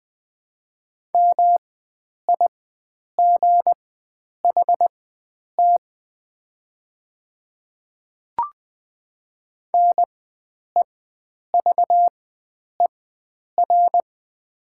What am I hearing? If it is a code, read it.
Morse code, heard as a keyed tone, about 20 wpm, spaced as for 9 wpm, 700 Hz. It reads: MIGHT NEVER
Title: 4x Character Spacing - Top 300 Words - 20wpm